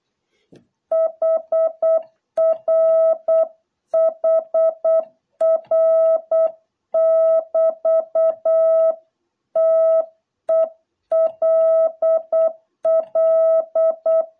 Morse code, heard as a keyed tone, about 8 wpm, 660 Hz. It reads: HR HR<BT>TELL